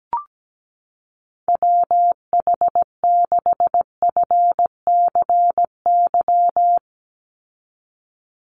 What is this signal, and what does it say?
Morse code, heard as a keyed tone, about 17 wpm, 700 Hz. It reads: WH6FCY
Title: USA Call Signs 17wpm